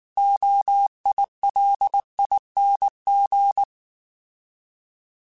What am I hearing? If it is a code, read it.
Morse code, heard as a keyed tone, about 19 wpm, 800 Hz. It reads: OILING
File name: Alphabetical / Alphabetical mp3s / Os @18